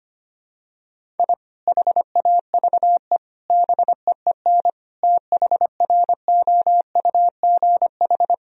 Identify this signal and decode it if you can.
Morse code, heard as a keyed tone, about 25 wpm, 700 Hz. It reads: I HAVE BEEN THROUGH
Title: Sentences from Top 1000 Words - Review Part 1 - 25wpm